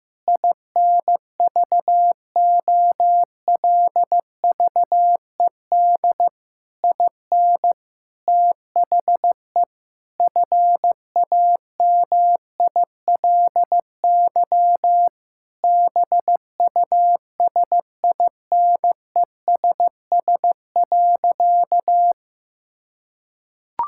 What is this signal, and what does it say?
Morse code, heard as a keyed tone, about 15 wpm, 700 Hz. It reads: INVOLVED IN THE FAMILY BUSINESS.
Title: Sentences from Top 1000 Words - Review Part 5 - 15wpm